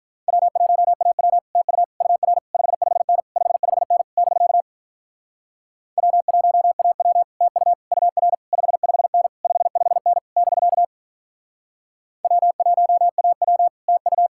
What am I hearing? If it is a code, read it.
Morse code, heard as a keyed tone, about 45 wpm, 700 Hz. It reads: W1AW TU UR 55N 55N <BK> W1AW TU UR 55N 55N <BK> W1AW TU